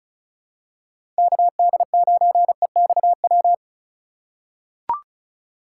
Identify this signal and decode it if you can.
Morse code, heard as a keyed tone, about 36 wpm, 700 Hz. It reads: KD9EXW
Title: USA Call Signs - 6 Characters - 3x Character Spacing - 35wpm